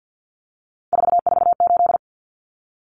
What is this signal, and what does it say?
Morse code, heard as a keyed tone, about 50 wpm, 700 Hz. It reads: 448
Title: Numbers - 3-Digit 50wpm